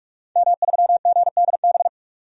Morse code, heard as a keyed tone, about 46 wpm, 700 Hz. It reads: M3GDB
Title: DXCC-100 Call Signs - 5 Characters - 45wpm